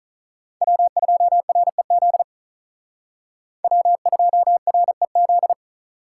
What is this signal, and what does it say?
Morse code, heard as a keyed tone, about 43 wpm, 700 Hz. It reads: W2REZ W2REZ
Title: Speed Racing - USA 1x3 Call Signs 35wpm